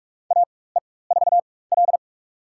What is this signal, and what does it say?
Morse code, heard as a keyed tone, about 42 wpm, spaced as for 20 wpm, 700 Hz. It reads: AE4L
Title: USA Call Signs - 4x Character Spacing - 45wpm